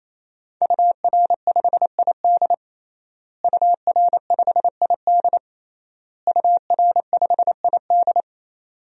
Morse code, heard as a keyed tone, about 28 wpm, 700 Hz. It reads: UR5ID UR5ID UR5ID
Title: DXCC-100 Call Signs - 5 Characters - Repeated 3x - 28wpm